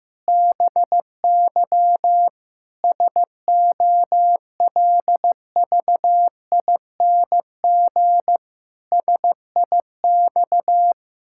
Morse code, heard as a keyed tone, about 15 wpm, 700 Hz. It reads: BY SOLVING SIX
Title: Pangram Challenge v2 - 15wpm